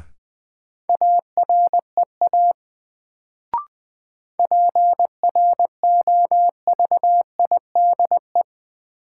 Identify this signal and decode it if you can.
Morse code, heard as a keyed tone, about 20 wpm, 700 Hz. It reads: AREA PROVIDE
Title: Sentences from Top 1000 Words - Review Part 2 - New Word Drill - 20wpm